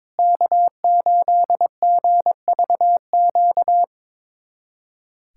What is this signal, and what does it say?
Morse code, heard as a keyed tone, about 22 wpm, 700 Hz. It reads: K8GVQ
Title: USA Callsign DE Callsign - 5 Characters - 22wpm